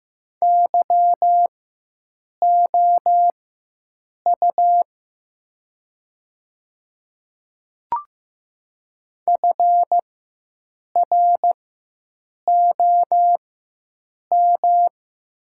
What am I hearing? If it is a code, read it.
Morse code, heard as a keyed tone, about 15 wpm, spaced as for 7 wpm, 700 Hz. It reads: YOU FROM
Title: 4x Character Spacing - Top 100 Words - 15wpm